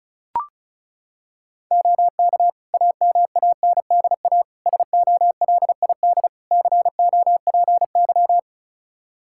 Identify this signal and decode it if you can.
Morse code, heard as a keyed tone, about 36 wpm, 700 Hz. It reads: OK AMANDA SOLID COPY
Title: K1USN SST QSO Practice 35wpm